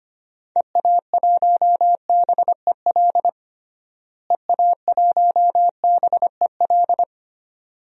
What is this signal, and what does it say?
Morse code, heard as a keyed tone, about 25 wpm, 700 Hz. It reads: EA1BEL EA1BEL
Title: DXCC-100 Call Signs - 6 Characters - Repeated 2x - 25wpm